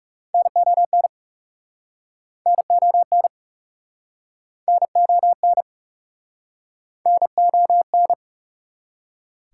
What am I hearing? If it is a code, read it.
Morse code, heard as a keyed tone, about 42 wpm, 700 Hz. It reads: NON NON NON NON